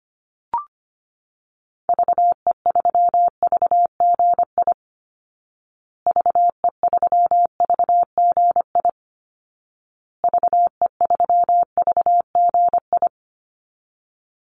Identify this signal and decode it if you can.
Morse code, heard as a keyed tone, about 25 wpm, 700 Hz. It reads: VE3VGI VE3VGI VE3VGI
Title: Call Signs Repeated 2x - 25wpm